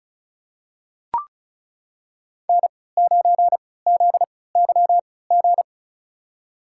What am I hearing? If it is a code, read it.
Morse code, heard as a keyed tone, about 36 wpm, spaced as for 20 wpm, 700 Hz. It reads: N9ZYG
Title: USA Call Signs - 5 Characters - 3x Character Spacing - 35wpm